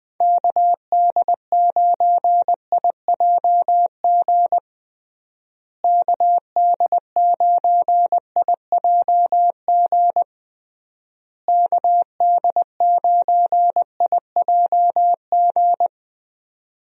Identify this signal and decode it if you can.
Morse code, heard as a keyed tone, about 20 wpm, 700 Hz. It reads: KD9IJG KD9IJG KD9IJG
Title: USA Call Signs - 6 Characters - Repeated 3x - 20wpm